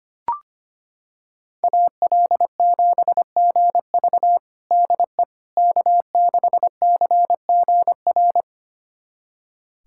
Morse code, heard as a keyed tone, about 25 wpm, 700 Hz. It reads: AL7GV DE K6CGR
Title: USA Callsign DE Callsign - 5 Characters - 25wpm